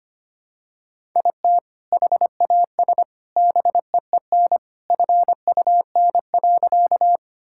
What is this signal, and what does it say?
Morse code, heard as a keyed tone, about 25 wpm, 700 Hz. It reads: IT HAS BEEN FUN.
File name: practice / 25wpm / Sentences with 3-Word Phrases - 25wpm